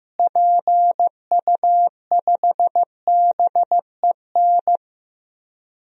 Morse code, heard as a keyed tone, about 15 wpm, 700 Hz. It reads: PU5BEN
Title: DXCC-100 Call Signs - 6 Characters - Repeated 3x - 15wpm